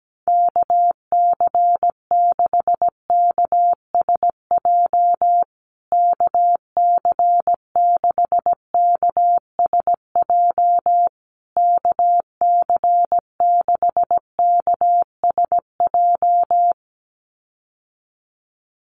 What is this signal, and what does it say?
Morse code, heard as a keyed tone, about 17 wpm, 700 Hz. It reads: KC6KSJ KC6KSJ KC6KSJ
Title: USA Call Signs Repeated 3x 17wpm